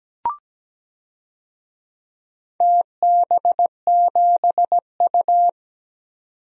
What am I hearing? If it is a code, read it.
Morse code, heard as a keyed tone, about 17 wpm, 700 Hz. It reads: TB7U
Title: DXCC-100 Call Signs - 4 Characters - 17wpm